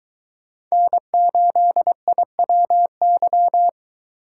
Morse code, heard as a keyed tone, about 23 wpm, 700 Hz. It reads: N8IWY